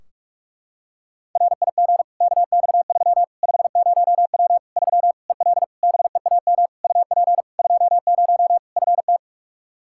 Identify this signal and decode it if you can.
Morse code, heard as a keyed tone, about 41 wpm, 700 Hz. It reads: RIG KX3 50W 3 EL BEAM UP 20 FT
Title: Practice QSO - Exchange 2 - Info - 45wpm